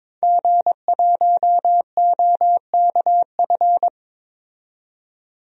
Morse code, heard as a keyed tone, about 23 wpm, 700 Hz. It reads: G1OKF
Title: DXCC-100 Call Signs - 5 Characters - 22wpm